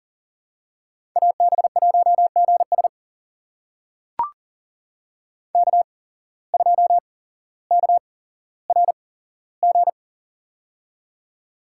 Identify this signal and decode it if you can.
Morse code, heard as a keyed tone, about 39 wpm, spaced as for 17 wpm, 700 Hz. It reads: AB1GS K2KRG
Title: USA Call Signs - 5 Characters - 8x Character Spacing - 40wpm